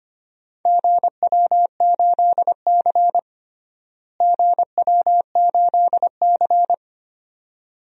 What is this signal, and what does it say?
Morse code, heard as a keyed tone, about 25 wpm, 700 Hz. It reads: GW8C GW8C
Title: DXCC-100 Call Signs - 4 Characters - Repeated 2x - 25wpm